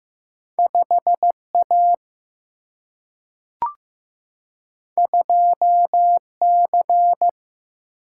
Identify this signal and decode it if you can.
Morse code, heard as a keyed tone, about 15 wpm, 700 Hz. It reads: 5A 2C
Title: Single-Letter Single-Number 15wpm